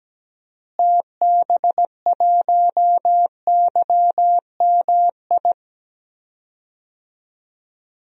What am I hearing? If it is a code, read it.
Morse code, heard as a keyed tone, about 17 wpm, 700 Hz. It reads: TB1YMI
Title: DXCC-100 Call Signs - 6 Characters - Repeated 2x - 17wpm